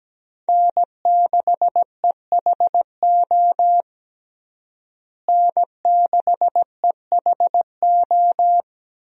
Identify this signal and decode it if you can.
Morse code, heard as a keyed tone, about 17 wpm, 700 Hz. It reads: N6EHO N6EHO
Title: USA Call Signs - 5 Characters - Repeated 2x - 17wpm